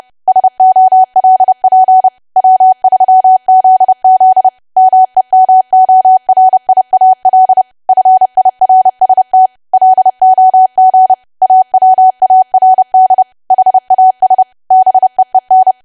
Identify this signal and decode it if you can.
Morse code, heard as a keyed tone, about 30 wpm, 750 Hz. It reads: SOLP W3ZZ MEMORIAL FIRST LOG AWARD HAS BEEN
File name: cwa / W1AW-Level3 / 30WPM